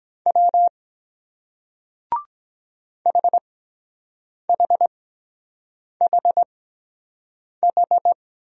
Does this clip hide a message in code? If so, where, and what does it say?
Yes: Morse code from 0.3 to 8.1 s, heard as a keyed tone, about 23 wpm, spaced as for 7 wpm, 700 Hz. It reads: W HHHH